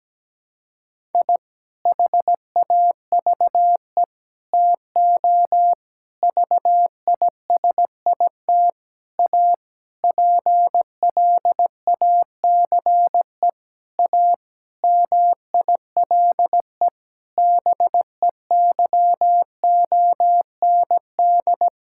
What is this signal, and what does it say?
Morse code, heard as a keyed tone, about 17 wpm, 700 Hz. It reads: I HAVE TO VISIT A PLACE A MILE BEYOND
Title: Sentences from Top 1000 Words - Review Part 6 - 17wpm